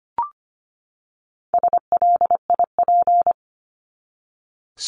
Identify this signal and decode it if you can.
Morse code, heard as a keyed tone, about 25 wpm, 700 Hz. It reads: SLIP
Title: Top 1751-2000 Words 25wpm